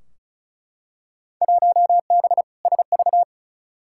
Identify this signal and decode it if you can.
Morse code, heard as a keyed tone, about 36 wpm, 690 Hz. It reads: 1B SV
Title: Field Day Exchanges Encore - 35wpm